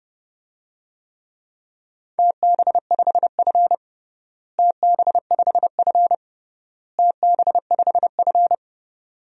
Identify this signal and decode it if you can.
Morse code, heard as a keyed tone, about 30 wpm, 700 Hz. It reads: TB5F TB5F TB5F
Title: DXCC-100 Call Signs - 4 Characters - Repeated 3x - 30wpm